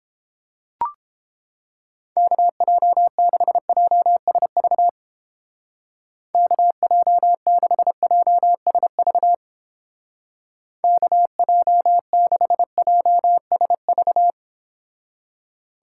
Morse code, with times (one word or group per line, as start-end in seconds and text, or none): 2.2-4.9 KJ6JSV
6.3-9.3 KJ6JSV
10.8-14.3 KJ6JSV